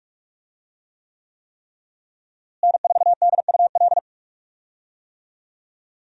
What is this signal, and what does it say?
Morse code, heard as a keyed tone, about 41 wpm, 700 Hz. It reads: N4DUL
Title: USA Call Signs - 5 Characters - Repeated 3x - 45wpm